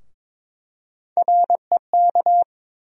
Morse code, heard as a keyed tone, about 22 wpm, 700 Hz. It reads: REK
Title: USA Call Sign Postfixes - 3 Letters 22wpm